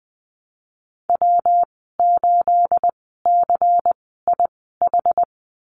Morse code, heard as a keyed tone, about 20 wpm, 700 Hz. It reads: W 8 C I H
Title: 238 Lesson 38g - Letter q - Call Signs 20wpm